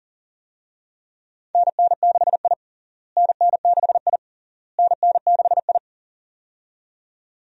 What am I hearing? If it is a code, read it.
Morse code, heard as a keyed tone, about 37 wpm, 700 Hz. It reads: NN6I NN6I NN6I
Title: USA Call Signs - 4 Characters - Repeated 3x - 40wpm